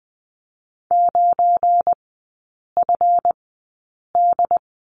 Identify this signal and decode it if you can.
Morse code, heard as a keyed tone, about 20 wpm, spaced as for 8 wpm, 690 Hz. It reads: 9FD
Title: 172 Lesson 28e - Number 9 - Three Characters 20wpm